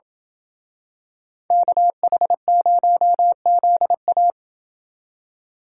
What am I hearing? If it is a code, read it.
Morse code, heard as a keyed tone, about 27 wpm, 700 Hz. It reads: KH0ZA